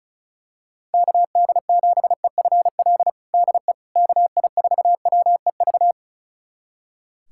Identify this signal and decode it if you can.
Morse code, heard as a keyed tone, about 36 wpm, 700 Hz. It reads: KD7EFL DE KI4WEV
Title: USA Callsign DE Callsign - 6 Characters - 35wpm